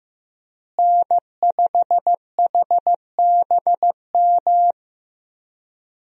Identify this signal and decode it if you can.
Morse code, heard as a keyed tone, about 16 wpm, 700 Hz. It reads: N5HBM
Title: USA Call Signs - 5 Characters - Repeated 3x - 15wpm